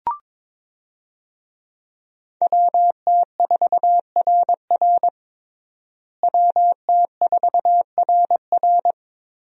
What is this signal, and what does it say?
Morse code, heard as a keyed tone, about 22 wpm, 700 Hz. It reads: WT4RR WT4RR
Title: USA Call Signs - 5 Characters - Repeated 2x - 22wpm